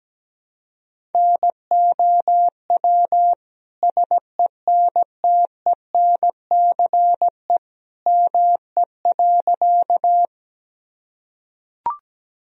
Morse code, heard as a keyed tone, about 17 wpm, 700 Hz. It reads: NOW SENTENCE ME.
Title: Sentences from Top 1000 Words - Review Part 1 - 17wpm